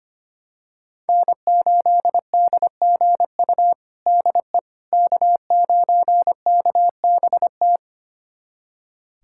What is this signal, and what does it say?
Morse code, heard as a keyed tone, about 25 wpm, 700 Hz. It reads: N8DGU DE K9KBT
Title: USA Callsign DE Callsign - 5 Characters - 25wpm